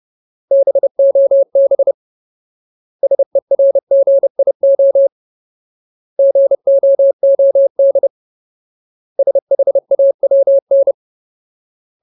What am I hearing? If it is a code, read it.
Morse code, heard as a keyed tone, about 30 wpm, 550 Hz. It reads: BOB SERGIO GOOD SHAWN
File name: Ultimate Word List / 30WPM Ultimate